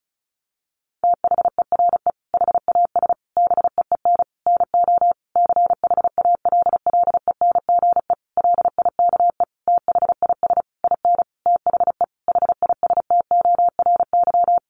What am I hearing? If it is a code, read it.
Morse code, heard as a keyed tone, about 34 wpm, 700 Hz. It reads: THERE HAS BEEN NO CHALLENGE LIKE THIS IN THE HISTORY
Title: Sentences from Top 1500 Words 35wpm